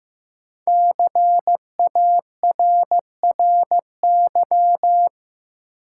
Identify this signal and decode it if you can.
Morse code, heard as a keyed tone, about 15 wpm, 700 Hz. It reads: CARRY
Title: Sentences from Top 400 Words - New Word Drill - 15wpm